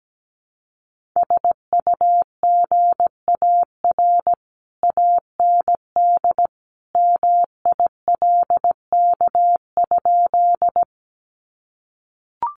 Morse code, heard as a keyed tone, about 17 wpm, 700 Hz. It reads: SUGAR AND MILK?
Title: Sentences from Top 1500 Words 17wpm